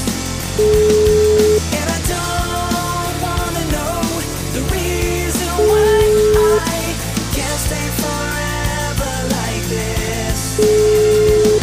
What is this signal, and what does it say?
Ringback tone (Europe, 425 Hz), with noise.